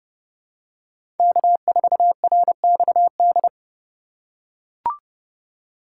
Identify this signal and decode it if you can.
Morse code, heard as a keyed tone, about 30 wpm, 700 Hz. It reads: K4RXD
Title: USA Call Signs - 5 Characters - 4x Character Spacing - 30wpm